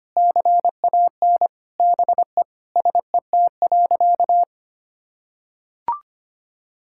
Morse code, heard as a keyed tone, about 25 wpm, 700 Hz. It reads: CAN BE SET.
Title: Sentences from Top 1000 Words - Review Part 6 - 25wpm